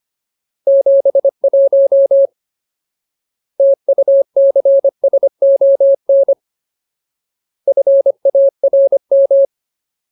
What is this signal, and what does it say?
Morse code, heard as a keyed tone, about 25 wpm, 550 Hz. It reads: 71 TUCSON FARM
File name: Ultimate Word List / 25WPM Ultimate